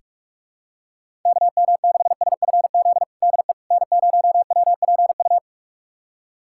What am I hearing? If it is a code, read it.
Morse code, heard as a keyed tone, about 46 wpm, 700 Hz. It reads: KM6SFZ DE N0WPU
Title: USA Callsign DE Callsign 45wpm